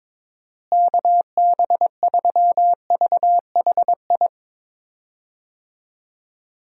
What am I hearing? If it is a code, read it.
Morse code, heard as a keyed tone, about 22 wpm, 700 Hz. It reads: KB3VHI
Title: USA Call Signs - 6 Characters - Repeated 2x - 22wpm